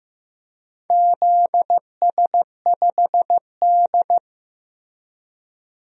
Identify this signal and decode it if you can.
Morse code, heard as a keyed tone, about 15 wpm, 700 Hz. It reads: ZS5D
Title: DXCC-100 Call Signs - 4 Characters - Repeated 3x - 15wpm